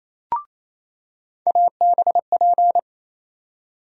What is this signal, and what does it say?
Morse code, heard as a keyed tone, about 28 wpm, 700 Hz. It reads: ABP
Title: USA Call Sign Postfixes - 3 Letters 28wpm